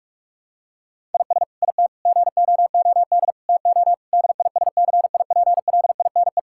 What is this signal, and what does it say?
Morse code, heard as a keyed tone, about 41 wpm, 700 Hz. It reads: IS IT GOOD TO DISCIPLINE
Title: Sentences from Top 1000 Words - Review Part 6 - 45wpm